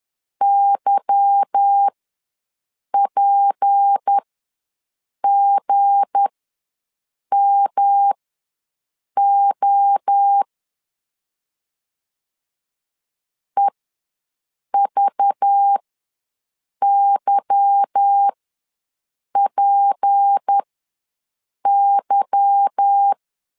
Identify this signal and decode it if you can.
Morse code, heard as a keyed tone, about 10 wpm, spaced as for 6 wpm, 800 Hz. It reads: YPGMO EVYPY
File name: LEZIONI / 8 YJ45 / 8 Y ws 10wpm 3952